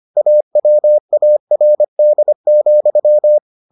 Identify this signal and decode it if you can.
Morse code, heard as a keyed tone, about 25 wpm, 600 Hz. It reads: AWARD,